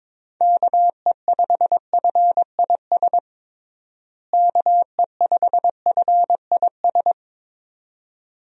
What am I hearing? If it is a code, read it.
Morse code, heard as a keyed tone, about 22 wpm, 700 Hz. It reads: KE5FIS KE5FIS
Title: USA Call Signs - 6 Characters - Repeated 2x - 22wpm